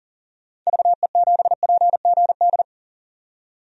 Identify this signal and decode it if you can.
Morse code, heard as a keyed tone, about 38 wpm, 700 Hz. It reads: VE7PGD
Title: DXCC-100 Call Signs - 6 Characters - 40wpm